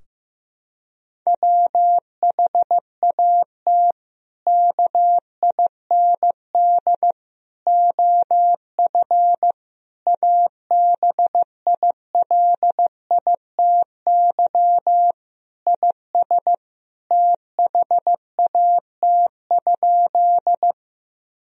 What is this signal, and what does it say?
Morse code, heard as a keyed tone, about 15 wpm, 700 Hz. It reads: WHAT KIND OF ABILITY IS THAT?
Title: Sentences from Top 1000 Words - Review Part 3 - 15wpm